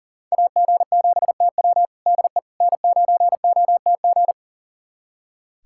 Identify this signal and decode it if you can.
Morse code, heard as a keyed tone, about 40 wpm, 700 Hz. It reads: AG7TW DE N9OTG
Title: USA Callsign DE Callsign - 5 Characters - 40wpm